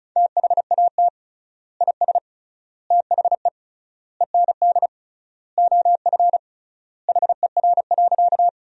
Morse code, heard as a keyed tone, about 35 wpm, 700 Hz. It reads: THAT IS THE END OF HER.
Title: 3x Word-Spacing - Sentences from Top 200 Words - 35wpm